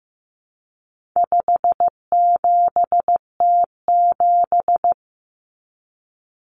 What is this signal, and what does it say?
Morse code, heard as a keyed tone, about 15 wpm, 700 Hz. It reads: 57T7
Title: Common Cut Numbers - 4-Digit 15wpm